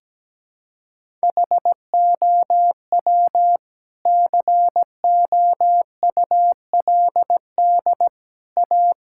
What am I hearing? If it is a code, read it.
Morse code, heard as a keyed tone, about 17 wpm, 700 Hz. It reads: HOW COULD A
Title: Sentences from Top 1000 Words - Review Part 6 - 17wpm